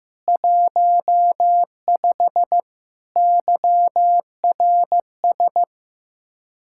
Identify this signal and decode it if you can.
Morse code, heard as a keyed tone, about 15 wpm, 700 Hz. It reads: 15 YRS
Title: Speed-Racing Common QSO Phrases Encore - 15wpm